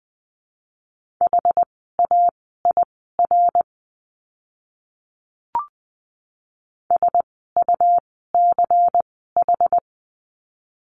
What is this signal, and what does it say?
Morse code, heard as a keyed tone, about 20 wpm, spaced as for 14 wpm, 700 Hz. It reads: HAIR SUCH
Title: 101 Lesson 17f - Letter u - Words 20wpm